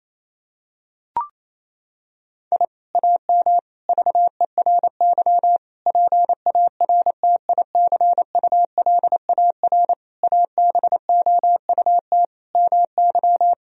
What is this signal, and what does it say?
Morse code, heard as a keyed tone, about 28 wpm, 700 Hz. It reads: I AM VERY PARTICULAR ABOUT MY
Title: Sentences from Top 1000 Words - Review Part 4 - 28wpm